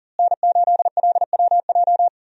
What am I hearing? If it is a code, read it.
Morse code, heard as a keyed tone, about 40 wpm, 700 Hz. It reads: N8LWJ
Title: USA Call Signs - 5 Characters - Repeated 3x - 40wpm